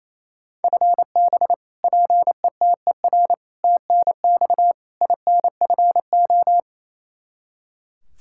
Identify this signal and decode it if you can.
Morse code, heard as a keyed tone, about 28 wpm, 700 Hz. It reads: FB PETER TNX INFO
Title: Practice QSO - Exchange 3 - More Info - 28wpm